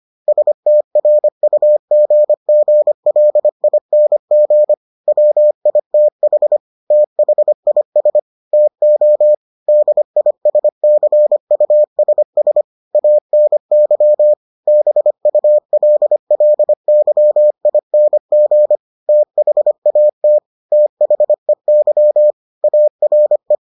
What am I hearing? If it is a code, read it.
Morse code, heard as a keyed tone, about 25 wpm, 600 Hz. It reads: STRUGGLING WITH THIS TO DISCUSS ANY BULLYING THAT THEY ARE